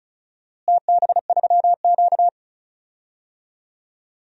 Morse code, heard as a keyed tone, about 34 wpm, 700 Hz. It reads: TB3Q
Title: DXCC-100 Call Signs - 4 Characters - Repeated 2x - 35wpm